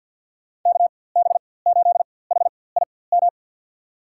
Morse code, heard as a keyed tone, about 53 wpm, spaced as for 24 wpm, 700 Hz. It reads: KB8HIM
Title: USA Call Signs - 4x Character Spacing - 50wpm